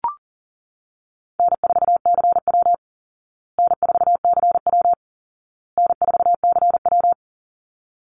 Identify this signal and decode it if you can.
Morse code, heard as a keyed tone, about 40 wpm, 700 Hz. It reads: N4CW N4CW N4CW
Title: Call Signs Repeated 2x - 40wpm